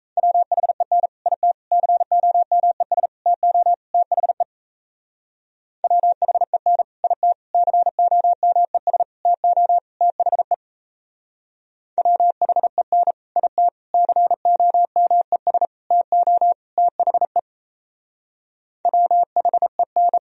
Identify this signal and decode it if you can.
Morse code, heard as a keyed tone, about 34 wpm, 700 Hz. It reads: WHEN IT COMES TO THE WHEN IT COMES TO THE WHEN IT COMES TO THE WHEN
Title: Speed-Racing - Sets of 5 Words 28wpm